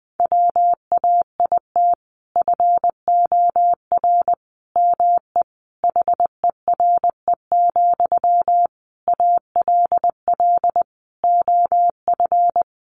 Morse code, heard as a keyed tone, about 20 wpm, 700 Hz. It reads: WAIT FOR ME HERE, ALL OF
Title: Sentences from Top 500 Words 20wpm